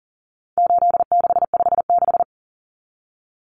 Morse code, heard as a keyed tone, about 40 wpm, 700 Hz. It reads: 8656
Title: Common Cut Numbers - 4-Digit 40wpm